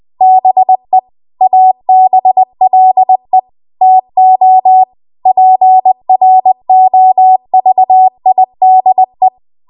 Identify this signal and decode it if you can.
Morse code, heard as a keyed tone, about 20 wpm, 750 Hz. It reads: BE ABLE TO PROVIDE